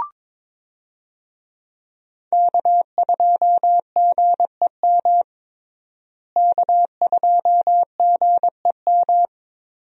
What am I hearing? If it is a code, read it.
Morse code, heard as a keyed tone, about 22 wpm, 700 Hz. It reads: K2GEM K2GEM
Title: USA Call Signs - 5 Characters - Repeated 2x - 22wpm